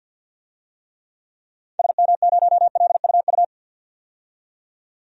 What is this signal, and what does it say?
Morse code, heard as a keyed tone, about 53 wpm, 700 Hz. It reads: SM0LUU
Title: DXCC-100 Call Signs - 6 Characters - 50wpm